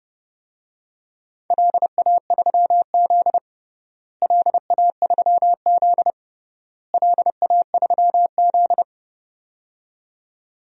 Morse code, heard as a keyed tone, about 32 wpm, 700 Hz. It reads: LA3Z LA3Z LA3Z
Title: DXCC-100 Call Signs - 4 Characters - Repeated 3x - 30wpm